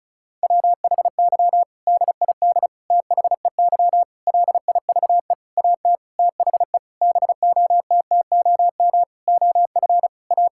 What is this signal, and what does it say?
Morse code, heard as a keyed tone, about 34 wpm, 700 Hz. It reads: WHY DID THEY LIVE AT THE BOTTOM OF A
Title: Sentences from Top 1000 Words - Review Part 6 - 35wpm